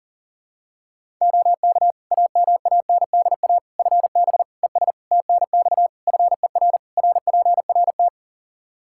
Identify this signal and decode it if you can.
Morse code, heard as a keyed tone, about 37 wpm, 700 Hz. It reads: OK AMANDA FB ES TNX FER RPRT
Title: K1USN SST QSO Practice 40wpm